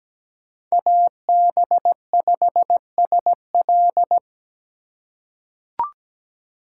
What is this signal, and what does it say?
Morse code, heard as a keyed tone, about 17 wpm, 700 Hz. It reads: AB5SL
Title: USA Call Signs - 5 Characters - 3x Character Spacing - 17wpm